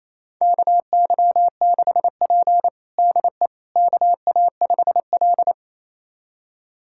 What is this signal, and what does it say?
Morse code, heard as a keyed tone, about 28 wpm, 700 Hz. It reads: KY6P DE KA5L
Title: USA Callsign DE Callsign - 4 Characters - 28wpm